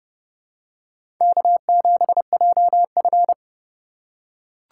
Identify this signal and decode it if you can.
Morse code, heard as a keyed tone, about 30 wpm, 700 Hz. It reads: K7JF
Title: USA Call Signs - 4 Characters - 8x Character Spacing - 30wpm